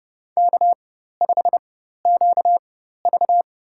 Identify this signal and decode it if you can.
Morse code, heard as a keyed tone, about 30 wpm, spaced as for 14 wpm, 700 Hz. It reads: K5QV
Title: USA Call Signs - 4 Characters - 4x Character Spacing - 30wpm